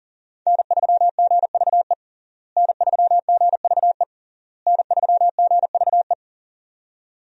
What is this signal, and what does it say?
Morse code, heard as a keyed tone, about 40 wpm, 700 Hz. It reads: N3GVE N3GVE N3GVE
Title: USA Call Signs - 5 Characters - Repeated 3x - 40wpm